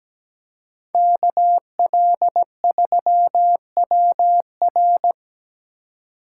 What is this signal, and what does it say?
Morse code, heard as a keyed tone, about 17 wpm, 700 Hz. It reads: KL3WR
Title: USA Call Signs - 5 Characters - 2x Character Spacing - 17wpm